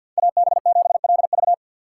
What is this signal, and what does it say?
Morse code, heard as a keyed tone, about 52 wpm, 700 Hz. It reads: AB7LV
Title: USA Call Signs - 5 Characters - 6x Character Spacing - 50wpm